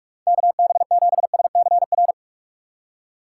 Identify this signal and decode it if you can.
Morse code, heard as a keyed tone, about 46 wpm, 690 Hz. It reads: KB7SCR
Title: USA Call Signs - 6 Characters - 3x Character Spacing - 45wpm